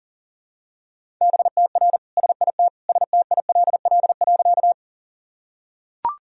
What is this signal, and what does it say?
Morse code, heard as a keyed tone, about 37 wpm, 700 Hz. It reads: BTR SIT STILL.
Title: Sentences from Top 500 Words and Abbreviations 40wpm